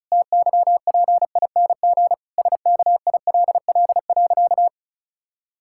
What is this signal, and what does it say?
Morse code, heard as a keyed tone, about 36 wpm, 700 Hz. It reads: TYPING SKILL.
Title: Pangram Challenge v2 - 35wpm